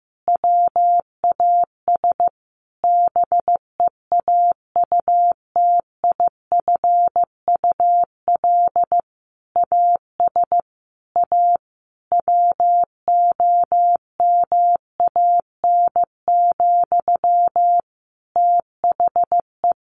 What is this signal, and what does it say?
Morse code, heard as a keyed tone, about 14 wpm, 700 Hz. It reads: WAS BEAUTIFUL AS A WOMAN, THE
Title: Sentences from Top 1500 Words 15wpm